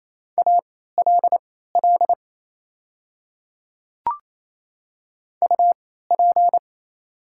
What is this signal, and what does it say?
Morse code, heard as a keyed tone, about 27 wpm, spaced as for 16 wpm, 700 Hz. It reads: ALL UP
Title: 3x Character Spacing - Top 100 Words - 28wpm